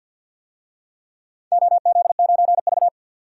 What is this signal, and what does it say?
Morse code, heard as a keyed tone, about 47 wpm, 700 Hz. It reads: OZ9V
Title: DXCC-100 Call Signs - 4 Characters - 50wpm